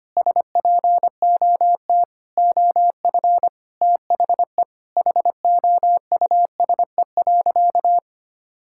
Morse code, heard as a keyed tone, about 25 wpm, 700 Hz. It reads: SPOT OF THE HOUSE.
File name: practice / 25wpm / Difficult Sentences with Hints - 25wpm